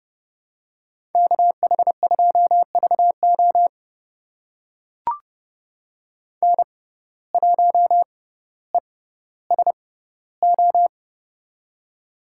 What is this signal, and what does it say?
Morse code, heard as a keyed tone, about 30 wpm, spaced as for 15 wpm, 700 Hz. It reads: KH2VO N1ESO
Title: USA Call Signs - 5 Characters - 6x Character Spacing - 30wpm